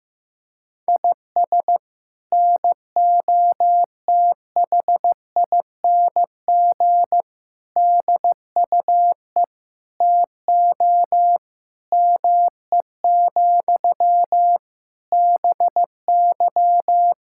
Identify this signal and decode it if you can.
Morse code, heard as a keyed tone, about 15 wpm, 700 Hz. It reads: IS NOTHING DUE TO ME, BY
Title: Sentences from Top 1000 Words - Review Part 6 - 15wpm